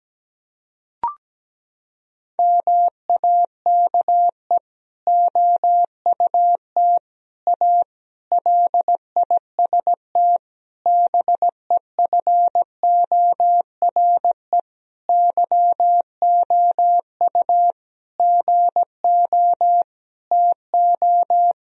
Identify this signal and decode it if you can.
Morse code, heard as a keyed tone, about 17 wpm, 700 Hz. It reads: MAKE OUT A LIST BEFORE YOU GO TO
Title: Sentences from Top 1000 Words - Review Part 2 - 17wpm